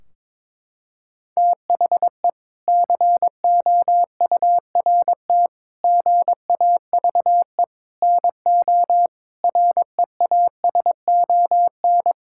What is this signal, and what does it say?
Morse code, heard as a keyed tone, about 22 wpm, 700 Hz. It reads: THE COURT GAVE NO REASON
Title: Sentences from Top 1600 Words 22wpm